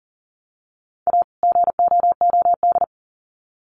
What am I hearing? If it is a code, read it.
Morse code, heard as a keyed tone, about 40 wpm, 700 Hz. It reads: A GOOD
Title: Sets of 2 Words 40wpm